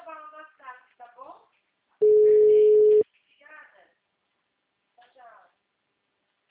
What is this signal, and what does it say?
Ringback tone (Europe, 425 Hz), with speech.